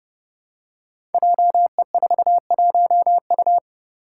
Morse code, heard as a keyed tone, about 28 wpm, 700 Hz. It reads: JE41U
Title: Five Challenging Letter-Number Combinations 30wpm